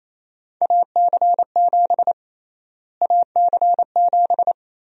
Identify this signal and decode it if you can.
Morse code, heard as a keyed tone, about 28 wpm, 700 Hz. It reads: AC7 AC7
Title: USA Call Sign Prefixes - Repeated 2x - 28wpm